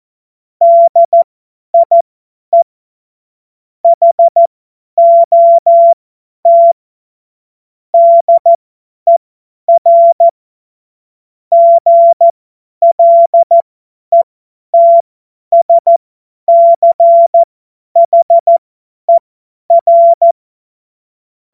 Morse code, heard as a keyed tone, about 14 wpm, spaced as for 10 wpm, 680 Hz. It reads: DIE HOT DER GLETSCHER